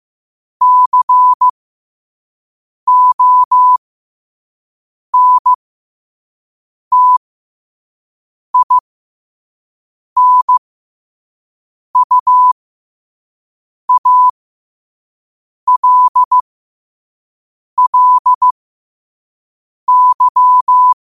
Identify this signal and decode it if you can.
Morse code, heard as a keyed tone, about 15 wpm, spaced as for 5 wpm, 1000 Hz. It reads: CONTINUALLY
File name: 5wpm